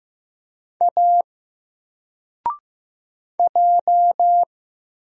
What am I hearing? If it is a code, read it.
Morse code, heard as a keyed tone, about 15 wpm, 700 Hz. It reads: A J
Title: Single Letter-Number 15wpm